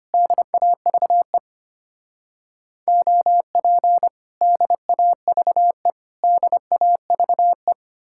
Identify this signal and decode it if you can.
Morse code, heard as a keyed tone, about 33 wpm, 700 Hz. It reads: DAVE OP DAVE DAVE